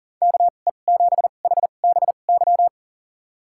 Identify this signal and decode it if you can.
Morse code, heard as a keyed tone, about 40 wpm, spaced as for 29 wpm, 700 Hz. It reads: KE7HBY
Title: USA Call Signs - 6 Characters - 2x Character Spacing - 40wpm